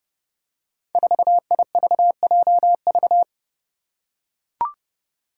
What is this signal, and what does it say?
Morse code, heard as a keyed tone, about 32 wpm, 700 Hz. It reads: 4IVJV
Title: Five Challenging Letter-Number Combinations 30wpm